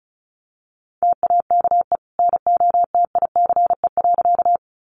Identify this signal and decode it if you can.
Morse code, heard as a keyed tone, about 34 wpm, 700 Hz. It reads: TAKE NOTICE.
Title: Sentences from Top 400 Words 35wpm